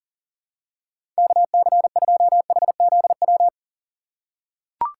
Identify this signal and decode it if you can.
Morse code, heard as a keyed tone, about 40 wpm, 700 Hz. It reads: KC2HZW
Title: USA Call Signs - 2x Character Spacing - 40wpm